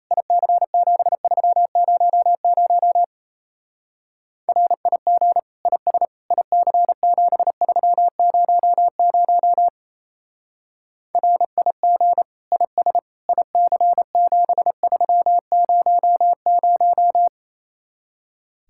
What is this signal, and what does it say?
Morse code, heard as a keyed tone, about 32 wpm, 700 Hz. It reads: IC7300 RIG IS IC7300 RIG IS IC7300
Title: Speed-Racing Common QSO Phrases Encore - 28wpm